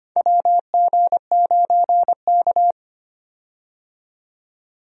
Morse code, heard as a keyed tone, about 25 wpm, 700 Hz. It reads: WG9K
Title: USA Call Signs - 4 Characters - 25wpm